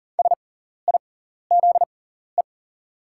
Morse code, heard as a keyed tone, about 40 wpm, spaced as for 14 wpm, 690 Hz. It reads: SIZE
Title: 6x Character Spacing - Top 500 Words - 40wpm